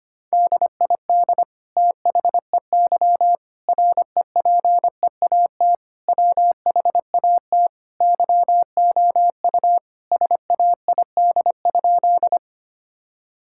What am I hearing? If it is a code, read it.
Morse code, heard as a keyed tone, about 25 wpm, 700 Hz. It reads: DID THEY REPEAT WHAT YOU SAID?
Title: Sentences from Top 1600 Words 25wpm